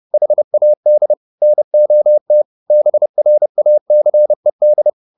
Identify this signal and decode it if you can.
Morse code, heard as a keyed tone, about 30 wpm, 600 Hz. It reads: HAD NOT BRACED
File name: the old man and the sea / the old man and the sea 1 30wpm